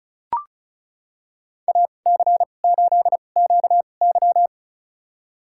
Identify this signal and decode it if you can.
Morse code, heard as a keyed tone, about 36 wpm, spaced as for 25 wpm, 700 Hz. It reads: AC8QY
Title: USA Call Signs - 2x Character Spacing - 35wpm